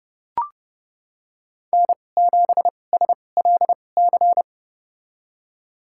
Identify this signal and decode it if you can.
Morse code, heard as a keyed tone, about 30 wpm, spaced as for 22 wpm, 700 Hz. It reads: N7SLC
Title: USA Call Signs - 5 Characters - 2x Character Spacing - 30wpm